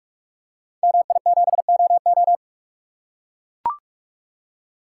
Morse code, heard as a keyed tone, about 44 wpm, 700 Hz. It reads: MI7OO
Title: Code Groups - 5 Letter:Numbers - 45wpm